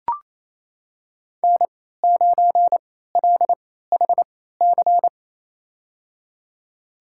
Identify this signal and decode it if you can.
Morse code, heard as a keyed tone, about 28 wpm, spaced as for 16 wpm, 700 Hz. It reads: N9LHC
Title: USA Call Signs - 3x Character Spacing - 28wpm